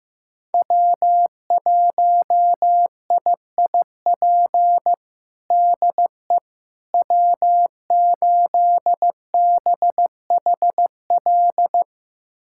Morse code, heard as a keyed tone, about 16 wpm, 700 Hz. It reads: W1IIP DE W8BHL
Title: USA Callsign DE Callsign - 5 Characters - 15wpm